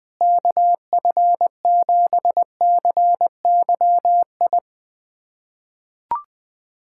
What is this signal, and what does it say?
Morse code, heard as a keyed tone, about 21 wpm, 700 Hz. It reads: KF7CYI
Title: USA Callsign DE Callsign - 6 Characters - 20wpm